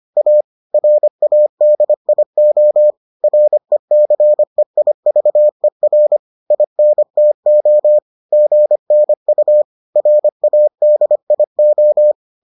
Morse code, heard as a keyed tone, about 25 wpm, 600 Hz. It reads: A RADIO RECEIVER INTO GNU RADIO